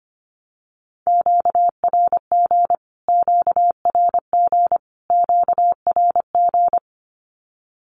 Morse code, heard as a keyed tone, about 25 wpm, 700 Hz. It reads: QRG QRG QRG
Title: Q-Codes Repeated 3x 25wpm